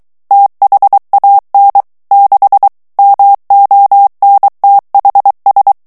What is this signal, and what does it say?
Morse code, heard as a keyed tone, about 23 wpm, 800 Hz. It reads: THAN 6 MONTHS
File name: ANGLAIS texte clair 3 vitesse 22 mots minute